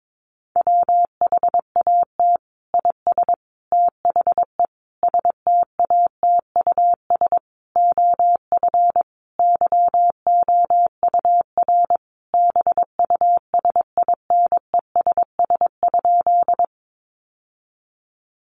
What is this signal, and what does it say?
Morse code, heard as a keyed tone, about 22 wpm, 700 Hz. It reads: WHAT IS THE STATUS OF YOUR BUSINESS?
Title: Sentences from Top 1400 Words 22wpm